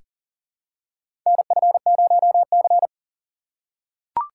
Morse code, heard as a keyed tone, about 37 wpm, 700 Hz. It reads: NF0C
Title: USA Call Signs - 4 Characters - 4x Character Spacing - 40wpm